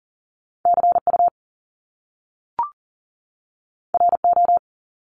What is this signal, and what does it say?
Morse code, heard as a keyed tone, about 40 wpm, 700 Hz. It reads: CU RO
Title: Top Two Letter Combinations in Words 40wpm